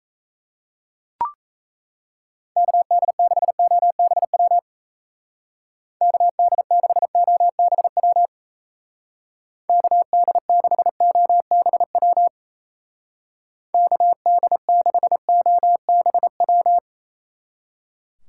Morse code, heard as a keyed tone, about 36 wpm, 700 Hz. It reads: KD6OBW KD6OBW KD6OBW KD6OBW